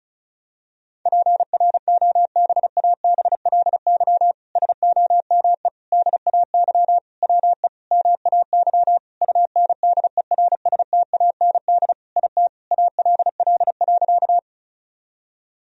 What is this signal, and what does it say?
Morse code, heard as a keyed tone, about 36 wpm, 700 Hz. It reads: PROBABLY SOME DAY WE MAY UNDERSTAND IT ALL.
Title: Sentences from Top 1000 Words - Review Part 4 - 35wpm